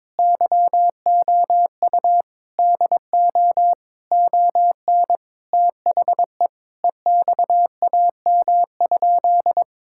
Morse code, heard as a keyed tone, about 22 wpm, 700 Hz. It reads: YOU DO ON THE EXAM?
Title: Sentences from Top 1000 Words - Review Part 5 - 22wpm